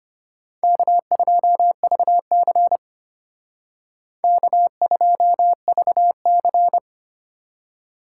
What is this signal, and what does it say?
Morse code, heard as a keyed tone, about 30 wpm, 700 Hz. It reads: K2VC K2VC
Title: Speed Racing - USA 1x2 Call Signs 25wpm